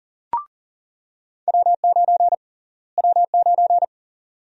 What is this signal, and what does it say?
Morse code, heard as a keyed tone, about 38 wpm, 700 Hz. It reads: W9 W9
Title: USA Call Sign Prefixes - Repeated 2x - 40wpm